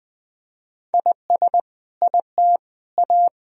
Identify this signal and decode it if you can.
Morse code, heard as a keyed tone, about 20 wpm, 700 Hz. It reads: IS IT A
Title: Sentences from Top 1000 Words - Review Part 2 - 20wpm